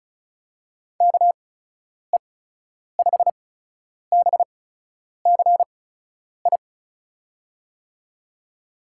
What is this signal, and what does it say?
Morse code, heard as a keyed tone, about 34 wpm, spaced as for 10 wpm, 700 Hz. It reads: KE5BCI
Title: USA Call Signs - 6 Characters - 8x Character Spacing - 35wpm